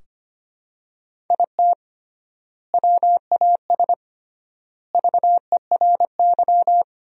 Morse code, heard as a keyed tone, about 25 wpm, 700 Hz. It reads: IT WAS VERY
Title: 3x Word-Spacing - Sentences from Top 300 Words - 25wpm